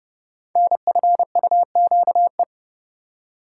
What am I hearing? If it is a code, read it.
Morse code, heard as a keyed tone, about 32 wpm, 700 Hz. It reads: NFUQE
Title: Code Groups - 5 Letter:Numbers - 30wpm